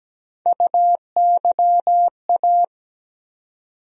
Morse code, heard as a keyed tone, about 17 wpm, 700 Hz. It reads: UYA
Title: USA Call Sign Postfixes - 3 Letters 17wpm